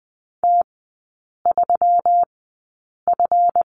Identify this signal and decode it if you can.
Morse code, heard as a keyed tone, about 20 wpm, spaced as for 8 wpm, 700 Hz. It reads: T3F
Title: 138 Lesson 23e - Letter f - Three Characters 20wpm